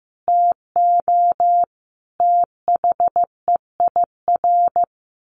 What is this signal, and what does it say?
Morse code, heard as a keyed tone, about 15 wpm, 700 Hz. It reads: TO THEIR
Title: A Sound Of Thunder 15wpm